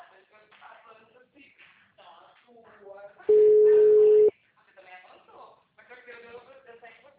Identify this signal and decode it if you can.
Ringback tone (Europe, 425 Hz), with speech.